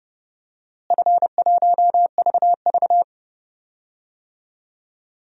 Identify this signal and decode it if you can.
Morse code, heard as a keyed tone, about 30 wpm, 700 Hz. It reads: F1VV
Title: DXCC-100 Call Signs - 4 Characters - Repeated 3x - 30wpm